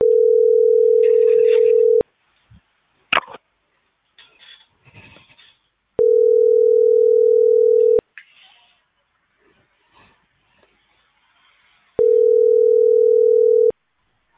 Ringback tone (North America), 440 + 480 Hz.